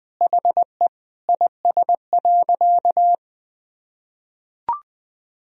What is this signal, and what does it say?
Morse code, heard as a keyed tone, about 20 wpm, 700 Hz. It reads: HE IS.
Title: Sentences from Top 200 Words and Abbreviations 20wpm